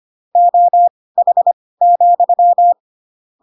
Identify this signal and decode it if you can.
Morse code, heard as a keyed tone, about 25 wpm, spaced as for 18 wpm, 700 Hz. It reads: OH,
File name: Pinocchio / Pinocchio ch7 12 25 ik1hge